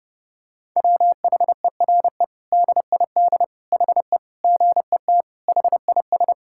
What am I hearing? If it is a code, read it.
Morse code, heard as a keyed tone, about 30 wpm, 700 Hz. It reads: WHERE DID HE GET HIS
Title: Sentences from Top 1000 Words - Review Part 5 - 30wpm